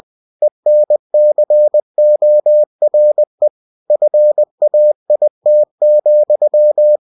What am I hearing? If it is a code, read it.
Morse code, heard as a keyed tone, about 20 wpm, 600 Hz. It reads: ENCORE FAIT,